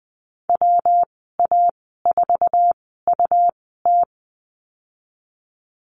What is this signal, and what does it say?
Morse code, heard as a keyed tone, about 20 wpm, spaced as for 15 wpm, 700 Hz. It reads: WA4UT
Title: 147 Lesson 24g - Letter y - Call Signs 20wpm